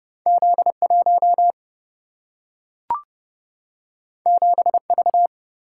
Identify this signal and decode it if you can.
Morse code, heard as a keyed tone, about 31 wpm, 700 Hz. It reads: Z 1 7 V
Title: Single-Letter Single-Number 30wpm